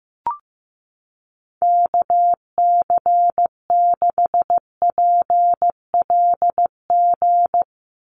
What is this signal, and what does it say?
Morse code, heard as a keyed tone, about 15 wpm, 700 Hz. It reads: KC6PLG
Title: USA Call Signs 15wpm